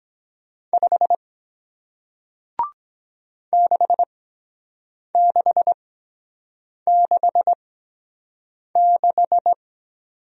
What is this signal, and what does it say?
Morse code, heard as a keyed tone, about 23 wpm, spaced as for 7 wpm, 700 Hz. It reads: 5 6666